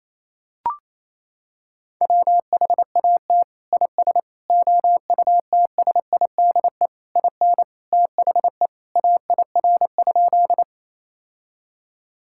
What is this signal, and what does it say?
Morse code, heard as a keyed tone, about 28 wpm, 700 Hz. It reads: WHAT IS OUTSIDE IN THE AIR?
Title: Sentences from Top 1000 Words - Review Part 6 - 28wpm